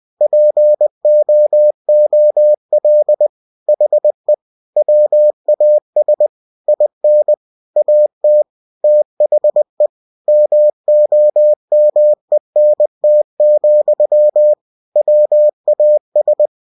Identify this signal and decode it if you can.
Morse code, heard as a keyed tone, about 20 wpm, 600 Hz. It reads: POOL HE WAS IN AT THE MOMENT, WAS
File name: winnie-20wpm